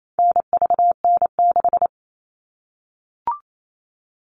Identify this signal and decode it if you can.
Morse code, heard as a keyed tone, about 28 wpm, 700 Hz. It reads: NVN6
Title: All Cut Numbers - 4-Digit 28wpm